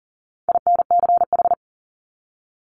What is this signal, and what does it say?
Morse code, heard as a keyed tone, about 39 wpm, 700 Hz. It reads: INCH
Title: Top 500 Words Repeated 40wpm